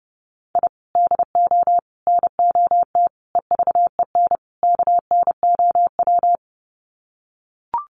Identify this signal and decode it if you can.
Morse code, heard as a keyed tone, about 30 wpm, 700 Hz. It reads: I DO NOT EVEN KNOW
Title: Sets of 5 Words 30wpm